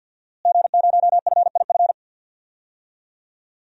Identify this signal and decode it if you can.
Morse code, heard as a keyed tone, about 50 wpm, 700 Hz. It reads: G0FIF
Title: DXCC-100 Call Signs - 5 Characters - Repeated 3x - 50wpm